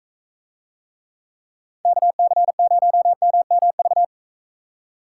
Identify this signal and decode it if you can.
Morse code, heard as a keyed tone, about 40 wpm, 700 Hz. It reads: KC0MMV